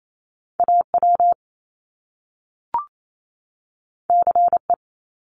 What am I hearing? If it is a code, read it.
Morse code, heard as a keyed tone, about 28 wpm, 700 Hz. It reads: AW CE
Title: Top Two Letter Combinations in Words 28wpm